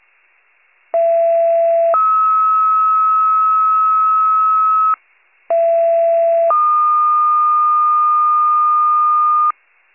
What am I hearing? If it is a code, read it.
Two-tone page, heard as a one-second tone, then a three-second tone, about 670 then 1290 Hz, about 670 then 1220 Hz. It is heard from 0.9 to 9.5 s.